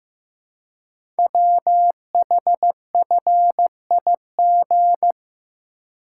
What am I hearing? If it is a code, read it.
Morse code, heard as a keyed tone, about 15 wpm, 700 Hz. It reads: WHFIG
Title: Code Groups - 5 Letters - 15wpm